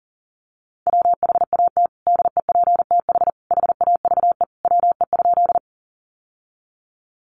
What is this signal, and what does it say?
Morse code, heard as a keyed tone, about 40 wpm, 700 Hz. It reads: WHAT DEPTH HAVE WE?
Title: Sentences from Top 1200 Words 40wpm